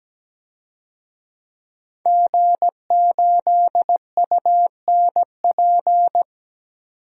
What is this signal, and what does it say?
Morse code, heard as a keyed tone, about 17 wpm, 700 Hz. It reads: G8UNP
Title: DXCC-100 Call Signs - 5 Characters - 17wpm